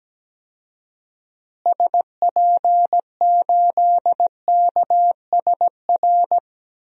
Morse code, heard as a keyed tone, about 17 wpm, 700 Hz. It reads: SP8KSR
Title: DXCC-100 Call Signs - 6 Characters - Repeated 3x - 17wpm